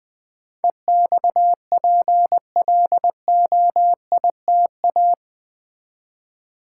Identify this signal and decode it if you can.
Morse code, heard as a keyed tone, about 20 wpm, 700 Hz. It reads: EXPLOITA
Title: Build A Word - 4 Letters At A Time - 12-Letter Words - 20wpm